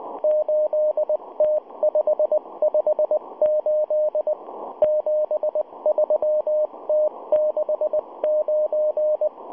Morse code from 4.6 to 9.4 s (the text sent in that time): 73T69